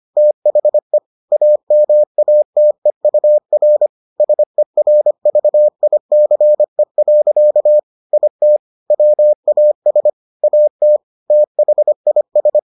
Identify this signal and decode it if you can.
Morse code, heard as a keyed tone, about 25 wpm, 600 Hz. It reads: THE AMATEUR SERVICE. IT WAS AT THIS